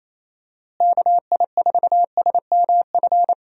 Morse code, heard as a keyed tone, about 28 wpm, 700 Hz. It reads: KI4SMF
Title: USA Call Signs - 6 Characters - Repeated 3x - 28wpm